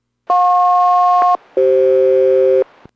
Aircraft SELCAL, heard as two pairs of tones. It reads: JP-CF